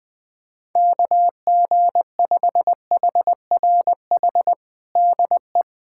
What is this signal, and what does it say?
Morse code, heard as a keyed tone, about 20 wpm, 700 Hz. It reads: KG5HRH DE
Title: USA Callsign DE Callsign - 6 Characters - 20wpm